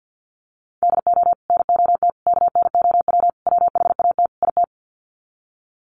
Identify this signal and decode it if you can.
Morse code, heard as a keyed tone, about 50 wpm, 700 Hz. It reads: DO NOT KNOW WHAT IT
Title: Sets of 5 Words 50wpm